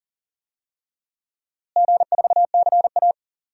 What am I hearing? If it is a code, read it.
Morse code, heard as a keyed tone, about 39 wpm, 700 Hz. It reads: G4CA
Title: DXCC-100 Call Signs - 4 Characters - Repeated 2x - 40wpm